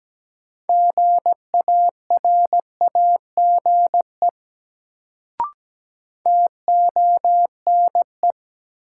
Morse code, heard as a keyed tone, about 17 wpm, 700 Hz. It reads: GARAGE TONE
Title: Sentences from Top 1600 Words - New Word Drill - 17wpm